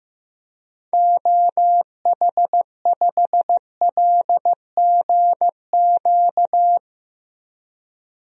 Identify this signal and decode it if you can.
Morse code, heard as a keyed tone, about 15 wpm, 700 Hz. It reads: OH5LGQ
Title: DXCC-100 Call Signs - 6 Characters - Repeated 2x - 15wpm